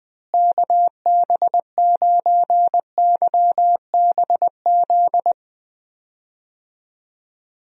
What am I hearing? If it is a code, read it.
Morse code, heard as a keyed tone, about 20 wpm, 700 Hz. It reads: KB9YBZ
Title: USA Call Signs - 6 Characters - Repeated 3x - 20wpm